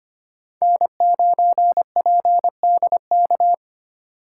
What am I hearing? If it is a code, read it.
Morse code, heard as a keyed tone, about 25 wpm, 700 Hz. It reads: N9PDK
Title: USA Call Signs - 5 Characters - 3x Character Spacing - 25wpm